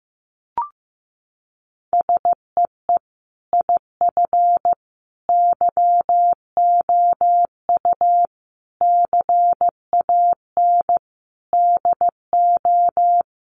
Morse code, heard as a keyed tone, about 15 wpm, 700 Hz. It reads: SEE IF YOU CAN DO
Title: Sentences from Top 1000 Words 15wpm